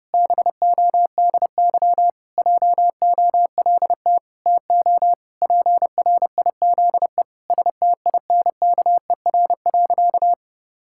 Morse code, heard as a keyed tone, about 32 wpm, 700 Hz. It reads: BODY JOLT TO PRIZE STINKER.
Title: Pangram Challenge v2 - 30wpm